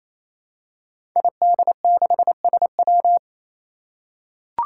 Morse code, heard as a keyed tone, about 27 wpm, 700 Hz. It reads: ID6SW
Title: Five Challenging Letter-Number Combinations 28wpm